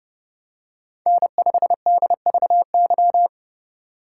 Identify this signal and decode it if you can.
Morse code, heard as a keyed tone, about 30 wpm, 700 Hz. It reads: N5DVY
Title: USA Call Signs - 4x Character Spacing - 30wpm